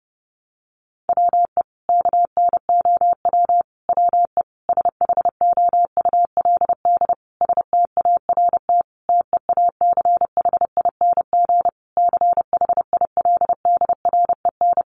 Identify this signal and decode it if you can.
Morse code, heard as a keyed tone, about 30 wpm, 700 Hz. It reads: WE KNOW WE SHOULD START TEACHING CHILDREN
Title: Sentences from Top 1100 Words 30wpm